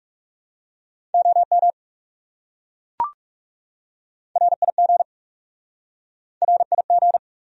Morse code, heard as a keyed tone, about 41 wpm, 700 Hz. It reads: OM RIG RIG